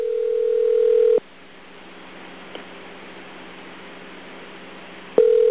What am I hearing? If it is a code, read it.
Ringback tone (North America), 440 + 480 Hz.